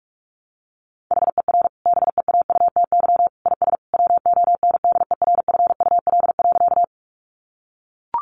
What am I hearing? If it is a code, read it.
Morse code, heard as a keyed tone, about 44 wpm, 700 Hz. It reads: HER BEAUTY IS WONDERFUL.
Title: Sentences from Top 1100 Words 45wpm